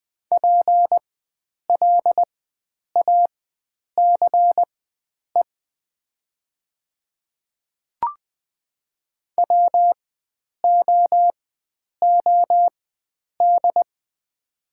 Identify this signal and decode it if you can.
Morse code, heard as a keyed tone, about 20 wpm, spaced as for 9 wpm, 700 Hz. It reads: PLACE WOOD
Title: 4x Character Spacing - Top 300 Words - 20wpm